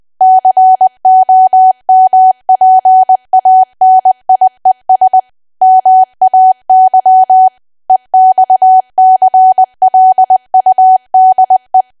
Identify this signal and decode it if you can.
Morse code, heard as a keyed tone, about 20 wpm, 750 Hz. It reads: COMPANIES MAY EXCLUDE
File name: cwa / W1AW-Level3 / 20WPM